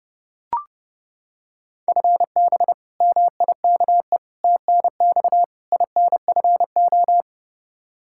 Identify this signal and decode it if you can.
Morse code, heard as a keyed tone, about 32 wpm, 700 Hz. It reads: FB MIKE TNX INFO
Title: Practice QSO - Exchange 3 - More Info - 30wpm